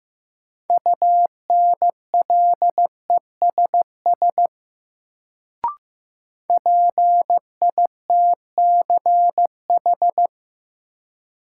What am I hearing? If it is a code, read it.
Morse code, heard as a keyed tone, about 15 wpm, 700 Hz. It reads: UNLESS PITCH
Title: Sentences from Top 1300 Words - New Word Drill Alt - 15wpm